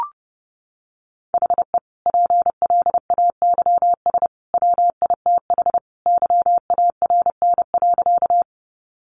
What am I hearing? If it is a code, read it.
Morse code, heard as a keyed tone, about 30 wpm, 700 Hz. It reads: HE PLAYS WITH YARN.
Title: Easy Phrases 30wpm